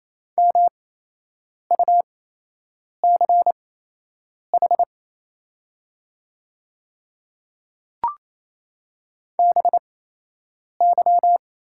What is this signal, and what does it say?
Morse code, heard as a keyed tone, about 28 wpm, spaced as for 8 wpm, 700 Hz. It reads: MUCH BY
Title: 8x Character Spacing - Top 200 Words - 28wpm